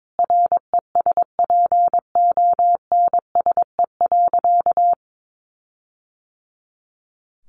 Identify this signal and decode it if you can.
Morse code, heard as a keyed tone, about 22 wpm, 700 Hz. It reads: RESPONSE.
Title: Sentences from Top 1400 Words 22wpm